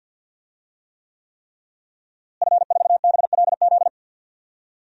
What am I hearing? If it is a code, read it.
Morse code, heard as a keyed tone, about 51 wpm, 700 Hz. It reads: F4BLZ
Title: DXCC-100 Call Signs - 5 Characters - Repeated 3x - 50wpm